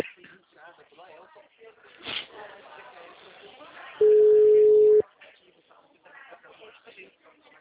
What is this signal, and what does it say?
Ringback tone (Europe, 425 Hz), with speech.